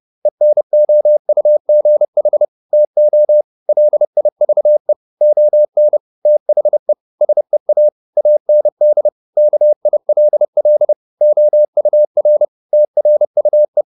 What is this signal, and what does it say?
Morse code, heard as a keyed tone, about 30 wpm, 600 Hz. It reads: ENOUGH TO LIVE ON THE SEA AND KILL OUR TRUE